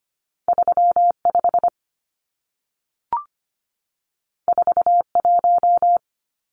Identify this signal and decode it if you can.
Morse code, heard as a keyed tone, about 25 wpm, 700 Hz. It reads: 35 41
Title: Numbers - 2-Digit 25wpm